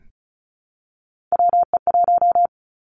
Morse code, heard as a keyed tone, about 36 wpm, 700 Hz. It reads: WE1
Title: Three Challenging Letter-Number Combinations 35wpm